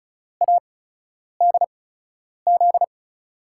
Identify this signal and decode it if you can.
Morse code, heard as a keyed tone, about 34 wpm, spaced as for 10 wpm, 700 Hz. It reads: ADZ
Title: USA Call Signs - 5 Characters - 8x Character Spacing - 35wpm